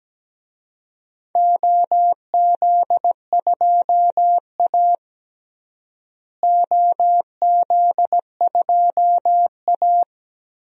Morse code, heard as a keyed tone, about 17 wpm, 700 Hz. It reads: OZ2A OZ2A
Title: DXCC-100 Call Signs - 4 Characters - Repeated 3x - 17wpm